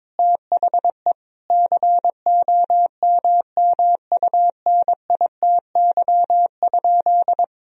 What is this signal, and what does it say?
Morse code, heard as a keyed tone, about 22 wpm, 700 Hz. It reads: THE COMMUNITY?
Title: Sentences from Top 1000 Words - Review Part 4 - 22wpm